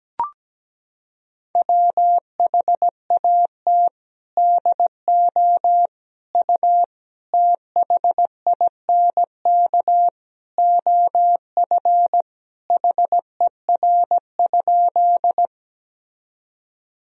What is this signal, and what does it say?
Morse code, heard as a keyed tone, about 17 wpm, 700 Hz. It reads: WHAT DO U THINK OF HER?
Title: Sentences from Top 200 Words and Abbreviations 17wpm